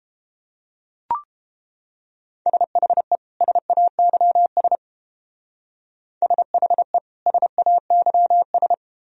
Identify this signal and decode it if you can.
Morse code, heard as a keyed tone, about 31 wpm, 700 Hz. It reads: SHE SAYS SHE SAYS